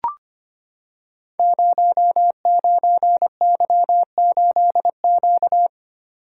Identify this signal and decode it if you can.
Morse code, heard as a keyed tone, about 25 wpm, 700 Hz. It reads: 09Y8Q
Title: Code Groups - 5 Letter:Numbers - 25wpm